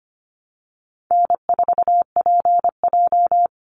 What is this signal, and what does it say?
Morse code, heard as a keyed tone, about 25 wpm, 700 Hz. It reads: N4PJ
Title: Easy Call Signs Repeated 3x - 25wpm